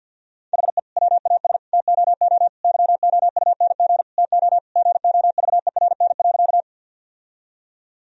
Morse code, heard as a keyed tone, about 50 wpm, 700 Hz. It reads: HE WAS TOO YOUNG TO GOVERN.
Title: Sentences from Top 1000 Words - Review Part 2 - 50wpm